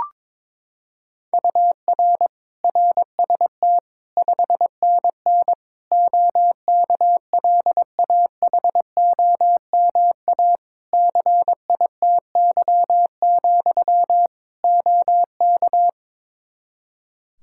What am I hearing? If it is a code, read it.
Morse code, heard as a keyed tone, about 22 wpm, 700 Hz. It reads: UR RST 5NN OKLAHOMA CITY, OK
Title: Practice QSO - Exchange 1 - Beginning - 22wpm